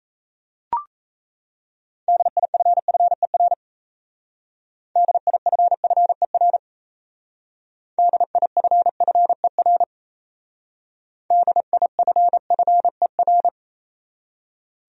Morse code, heard as a keyed tone, about 33 wpm, 700 Hz. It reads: DIFFER DIFFER DIFFER DIFFER
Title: Speed-Racing Top 200 Words 28wpm